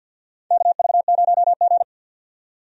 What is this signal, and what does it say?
Morse code, heard as a keyed tone, about 52 wpm, 700 Hz. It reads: KV0G
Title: USA Call Signs - 4 Characters - 50wpm